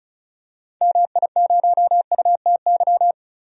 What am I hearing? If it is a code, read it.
Morse code, heard as a keyed tone, about 34 wpm, 700 Hz. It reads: MI0UTY
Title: DXCC-100 Call Signs - 6 Characters - 35wpm